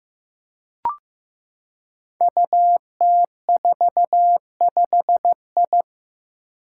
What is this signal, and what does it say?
Morse code, heard as a keyed tone, about 15 wpm, 700 Hz. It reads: UT45I
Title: Five Challenging Letter-Number Combinations 15wpm